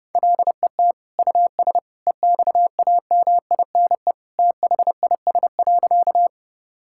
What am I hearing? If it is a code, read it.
Morse code, heard as a keyed tone, about 30 wpm, 700 Hz. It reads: LET US EXAMINE THIS.
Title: Sentences from Top 1600 Words 30wpm